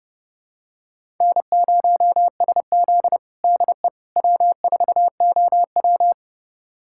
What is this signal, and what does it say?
Morse code, heard as a keyed tone, about 30 wpm, 700 Hz. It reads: N0SZ DE W4OW
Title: USA Callsign DE Callsign - 4 Characters - 30wpm